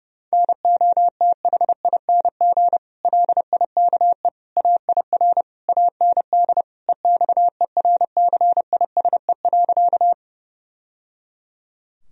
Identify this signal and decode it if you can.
Morse code, heard as a keyed tone, about 32 wpm, 700 Hz. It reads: NOTHING LIKE AIR AND EXERCISE.
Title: Sentences from Top 1000 Words - Review Part 4 - 30wpm